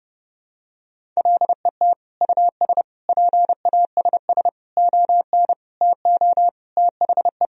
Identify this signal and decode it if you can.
Morse code, heard as a keyed tone, about 30 wpm, 700 Hz. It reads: LET US PASS ON TO THE
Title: Sentences from Top 1000 Words - Review Part 1 - 30wpm